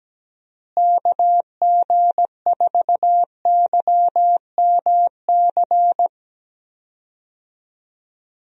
Morse code, heard as a keyed tone, about 17 wpm, 700 Hz. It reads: KG4YMC
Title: USA Call Signs - 6 Characters - Repeated 2x - 17wpm